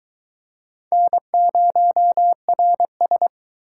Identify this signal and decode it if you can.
Morse code, heard as a keyed tone, about 22 wpm, 700 Hz. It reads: N0RS